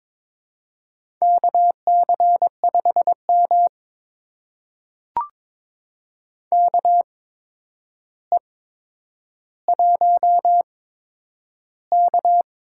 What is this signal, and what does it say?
Morse code, heard as a keyed tone, about 22 wpm, spaced as for 9 wpm, 700 Hz. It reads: KC5M KE1K
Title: USA Call Signs - 4 Characters - 8x Character Spacing - 22wpm